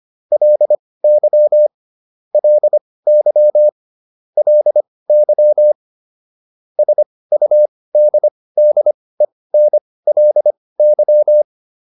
Morse code, heard as a keyed tone, about 25 wpm, spaced as for 18 wpm, 600 Hz. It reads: LY LY LY SUDDENLY